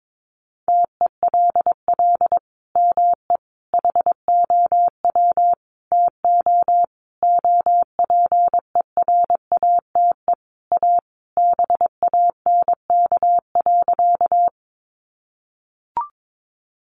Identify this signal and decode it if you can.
Morse code, heard as a keyed tone, about 22 wpm, 700 Hz. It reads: TELL ME HOW TO OPERATE A BANK.
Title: Sentences from Top 1500 Words 22wpm